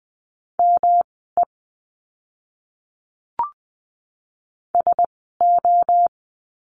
Morse code, heard as a keyed tone, about 20 wpm, spaced as for 14 wpm, 700 Hz. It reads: ME SO
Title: 230 Lesson 37f - Letter x - Words 20wpm